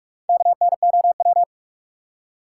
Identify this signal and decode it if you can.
Morse code, heard as a keyed tone, about 60 wpm, spaced as for 52 wpm, 700 Hz. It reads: KNOW
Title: Speed-Racing Top 180 QSO Elements 45wpm